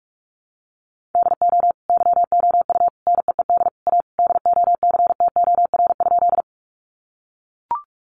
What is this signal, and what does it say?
Morse code, heard as a keyed tone, about 44 wpm, 700 Hz. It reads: DO YOU NEED A DOCTOR?
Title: Sentences from Top 1500 Words 45wpm